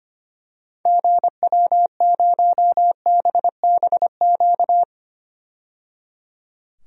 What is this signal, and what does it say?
Morse code, heard as a keyed tone, about 25 wpm, 700 Hz. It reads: GW0BBQ
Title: DXCC-100 Call Signs - 6 Characters - 25wpm